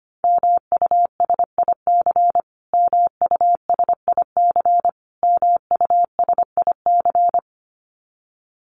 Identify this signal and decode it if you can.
Morse code, heard as a keyed tone, about 25 wpm, 700 Hz. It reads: MUSIC MUSIC MUSIC
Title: Top 500 Words Repeated 25wpm